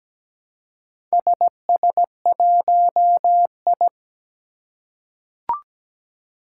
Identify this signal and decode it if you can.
Morse code, heard as a keyed tone, about 17 wpm, 700 Hz. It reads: SS1I
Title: Four Challenging Letter-Number Combinations 17wpm